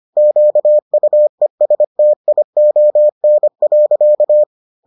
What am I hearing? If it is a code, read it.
Morse code, heard as a keyed tone, about 25 wpm, 600 Hz. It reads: QUESTION.